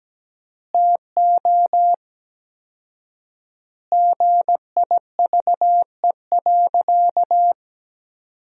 Morse code, heard as a keyed tone, about 17 wpm, 700 Hz. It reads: TO GIVE.
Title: 4x Word-Spacing - Sentences from Top 200 Words - 17wpm